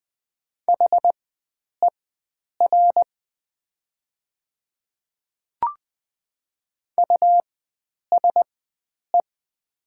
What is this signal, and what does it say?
Morse code, heard as a keyed tone, about 20 wpm, spaced as for 9 wpm, 690 Hz. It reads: HER USE
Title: 4x Character Spacing - Top 100 Words - 20wpm